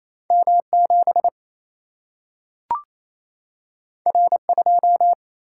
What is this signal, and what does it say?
Morse code, heard as a keyed tone, about 28 wpm, 700 Hz. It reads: M7 R2
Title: Single-Letter Single-Number 28wpm